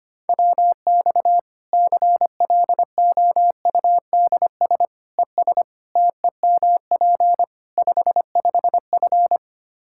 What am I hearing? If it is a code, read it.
Morse code, heard as a keyed tone, about 25 wpm, 700 Hz. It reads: WX CLOUDS ES TEMP 55F
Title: Practice QSO - Exchange 2 - Info - 25wpm